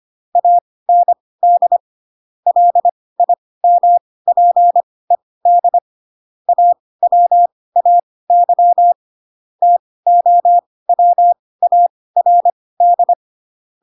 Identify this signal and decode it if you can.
Morse code, heard as a keyed tone, about 25 wpm, spaced as for 18 wpm, 700 Hz. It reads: AND LIMPED AWAY TOWARD
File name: Pinocchio / Pinocchio ch2 12 25 ik1hge